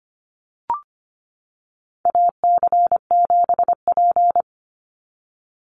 Morse code, heard as a keyed tone, about 25 wpm, 700 Hz. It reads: AC7P
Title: NA SOTA Chaser Callsigns 25wpm